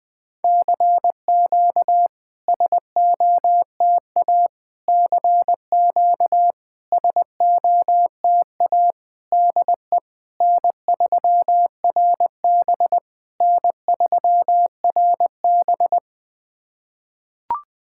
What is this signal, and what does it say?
Morse code, heard as a keyed tone, about 20 wpm, 690 Hz. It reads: CQ SOTA CQ SOTA DE N3RB N3RB
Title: SOTA QSO Practice 20wpm